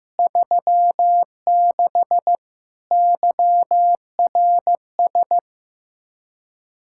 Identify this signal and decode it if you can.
Morse code, heard as a keyed tone, about 15 wpm, 700 Hz. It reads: 36 YRS
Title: Practice QSO - Exchange 3 - More Info - 15wpm